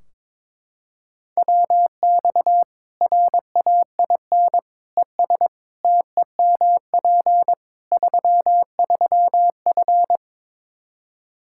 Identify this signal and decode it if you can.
Morse code, heard as a keyed tone, about 22 wpm, 700 Hz. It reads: WX RAIN ES TEMP 33F
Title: Practice QSO - Exchange 2 - Info - 22wpm